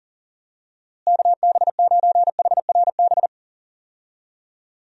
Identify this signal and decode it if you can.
Morse code, heard as a keyed tone, about 40 wpm, 700 Hz. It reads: KB9HRB
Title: USA Call Signs - 6 Characters - Repeated 2x - 40wpm